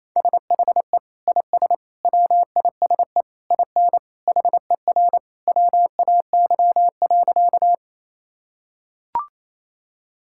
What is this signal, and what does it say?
Morse code, heard as a keyed tone, about 28 wpm, 700 Hz. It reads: SHE IS WISE IN HER WAY.
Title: Sentences from Top 1600 Words 28wpm